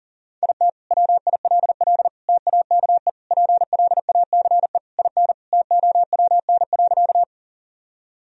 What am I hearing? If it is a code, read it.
Morse code, heard as a keyed tone, about 40 wpm, 700 Hz. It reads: IT WILL TAKE PLACE IN TOWN.
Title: Sentences from Top 1000 Words - Review Part 2 - 40wpm